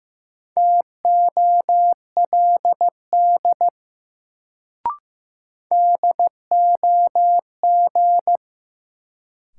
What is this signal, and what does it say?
Morse code, heard as a keyed tone, about 15 wpm, 700 Hz. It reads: TOLD DOG
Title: Sentences from Top 400 Words - New Word Drill - 15wpm